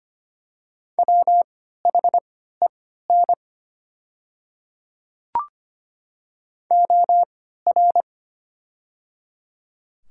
Morse code, heard as a keyed tone, about 25 wpm, spaced as for 14 wpm, 700 Hz. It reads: WHEN OR
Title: 3x Character Spacing - Top 500 Words - 25wpm